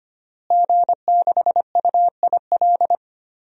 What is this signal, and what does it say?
Morse code, heard as a keyed tone, about 25 wpm, 700 Hz. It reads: G6UIL
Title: DXCC-100 Call Signs - 5 Characters - Repeated 3x - 25wpm